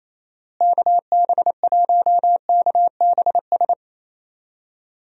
Morse code, heard as a keyed tone, about 28 wpm, 700 Hz. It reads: KB1KBS
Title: USA Call Signs - 6 Characters - Repeated 3x - 28wpm